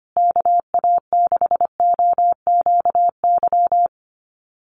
Morse code, heard as a keyed tone, about 25 wpm, 700 Hz. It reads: KA6OQY
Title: USA Call Signs Repeated 2x 25wpm